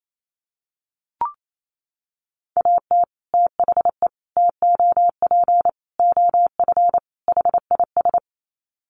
Morse code, heard as a keyed tone, about 28 wpm, 700 Hz. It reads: AT THE TOP OF HIS
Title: Sets of 5 Words Encore 28wpm